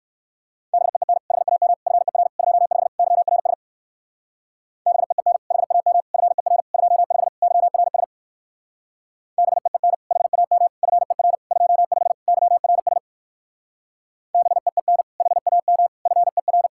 Morse code, heard as a keyed tone, about 64 wpm, 700 Hz. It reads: BEEN HAM FER 25 YRS BEEN HAM FER 25 YRS BEEN HAM FER 25 YRS BEEN HAM FER